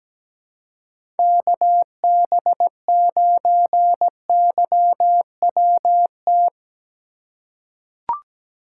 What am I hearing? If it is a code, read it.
Morse code, heard as a keyed tone, about 17 wpm, 700 Hz. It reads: KB9YWT
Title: USA Call Signs - 6 Characters - 2x Character Spacing - 17wpm